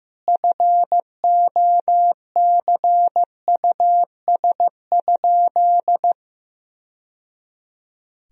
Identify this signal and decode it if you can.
Morse code, heard as a keyed tone, about 15 wpm, 700 Hz. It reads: FOCUS?
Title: Sentences from Top 1000 Words - Review Part 4 - 15wpm